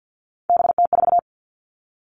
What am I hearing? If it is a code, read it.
Morse code, heard as a keyed tone, about 50 wpm, 700 Hz. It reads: BT4
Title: Three Challenging Letter-Number Combinations 50wpm